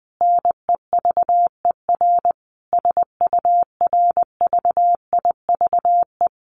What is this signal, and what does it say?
Morse code, heard as a keyed tone, about 20 wpm, 700 Hz. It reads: NEVER SURVIVE
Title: Sentences from Top 1300 Words 20wpm